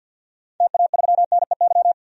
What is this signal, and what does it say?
Morse code, heard as a keyed tone, about 52 wpm, 700 Hz. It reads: TA3NEY
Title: DXCC-100 Call Signs - 6 Characters - 50wpm